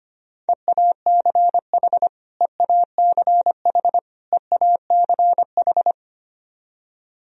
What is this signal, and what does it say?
Morse code, heard as a keyed tone, about 25 wpm, 700 Hz. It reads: EACH EACH EACH
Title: Top 100 Words Repeated 25wpm